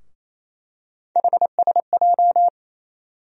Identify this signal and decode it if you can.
Morse code, heard as a keyed tone, about 28 wpm, 700 Hz. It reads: HSJ
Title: USA Call Sign Postfixes - 3 Letters 28wpm